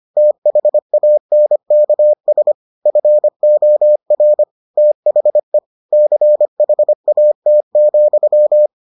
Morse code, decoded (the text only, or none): THANKS FOR THE CHAT,